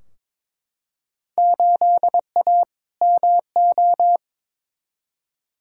Morse code, heard as a keyed tone, about 22 wpm, 700 Hz. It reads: 8A MO
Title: Field Day Exchanges Encore - 22wpm